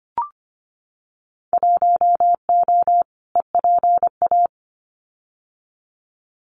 Morse code, heard as a keyed tone, about 25 wpm, 690 Hz. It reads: 1O EPA
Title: Winter Field Day Exchanges 25wpm